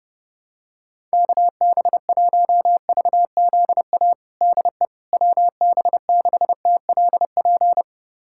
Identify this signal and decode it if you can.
Morse code, heard as a keyed tone, about 32 wpm, 700 Hz. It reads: KB1VZA DE WB6TLP
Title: USA Callsign DE Callsign - 6 Characters - 30wpm